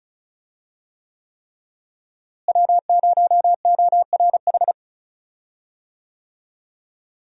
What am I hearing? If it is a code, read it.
Morse code, heard as a keyed tone, about 34 wpm, 700 Hz. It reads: W0ORH
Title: USA Call Signs - 5 Characters - Repeated 3x - 35wpm